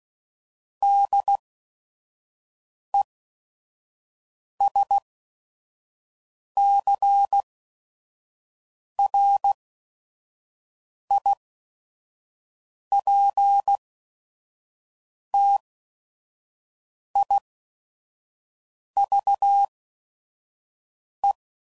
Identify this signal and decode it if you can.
Morse code, heard as a keyed tone, about 16 wpm, spaced as for 5 wpm, 800 Hz. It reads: DESCRIPTIVE
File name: CW Code Groups @5WPM / Technical Subjects / Reactance